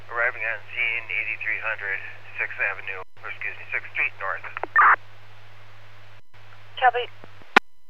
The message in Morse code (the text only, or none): none